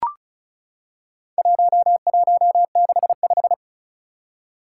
Morse code, heard as a keyed tone, about 36 wpm, 700 Hz. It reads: 1165
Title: Four Challenging Letter-Number Combinations 35wpm